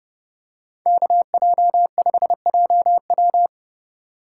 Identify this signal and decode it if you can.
Morse code, heard as a keyed tone, about 29 wpm, 700 Hz. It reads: KJ5JW